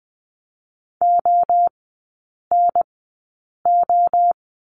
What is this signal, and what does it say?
Morse code, heard as a keyed tone, about 20 wpm, spaced as for 8 wpm, 700 Hz. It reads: ONO
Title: 074 Lesson 13e - Letter L - Three Characters 20wpm